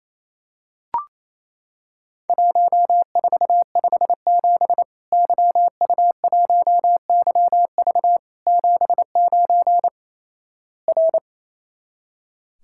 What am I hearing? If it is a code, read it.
Morse code, heard as a keyed tone, about 28 wpm, 700 Hz. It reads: 1457 YU1YV 79 R
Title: Worked All Europe DX Contest - QTC Exchange - 28wpm